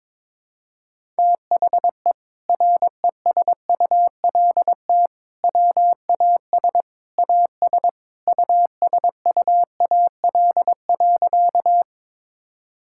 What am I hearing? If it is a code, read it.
Morse code, heard as a keyed tone, about 22 wpm, 700 Hz. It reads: THE RESULT WAS AS USUAL.
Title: Sentences from Top 1000 Words - Review Part 5 - 22wpm